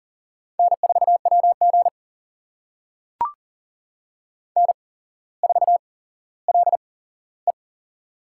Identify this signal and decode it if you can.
Morse code, heard as a keyed tone, about 40 wpm, spaced as for 17 wpm, 700 Hz. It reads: N4WG N4LE
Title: USA Call Signs - 4 Characters - 8x Character Spacing - 40wpm